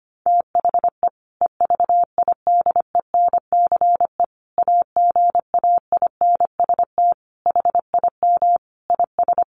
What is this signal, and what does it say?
Morse code, heard as a keyed tone, about 25 wpm, 700 Hz. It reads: THE EVIDENCE AGAINST HIM IS
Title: Sentences from Top 1300 Words 25wpm